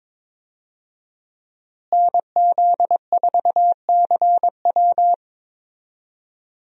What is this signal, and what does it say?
Morse code, heard as a keyed tone, about 21 wpm, 700 Hz. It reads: NZ4CW
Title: USA Call Signs - 5 Characters - Repeated 2x - 22wpm